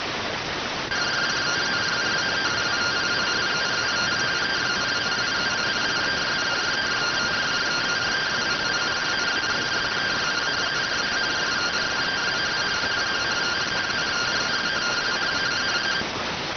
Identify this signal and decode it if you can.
Two-tone teleprinter signal, about 170 Hz shift.